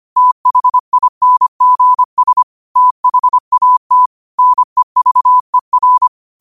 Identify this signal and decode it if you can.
Morse code, heard as a keyed tone, about 25 wpm, 1000 Hz. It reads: THINGS THAT NEVER
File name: qotdfiles / 25wpm